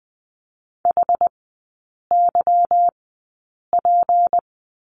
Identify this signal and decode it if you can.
Morse code, heard as a keyed tone, about 20 wpm, spaced as for 8 wpm, 700 Hz. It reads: HYP
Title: 152 Lesson 25e - Letter p - Three Characters 20wpm